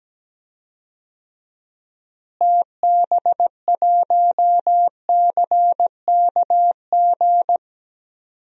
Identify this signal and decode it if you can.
Morse code, heard as a keyed tone, about 17 wpm, 700 Hz. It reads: TB1CKG
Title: DXCC-100 Call Signs - 6 Characters - Repeated 3x - 17wpm